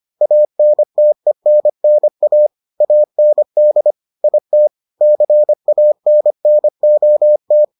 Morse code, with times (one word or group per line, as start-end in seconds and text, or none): 0.2-2.5 ANTENNA
2.8-3.9 AND
4.2-4.7 IT
5.0-7.6 CANNOT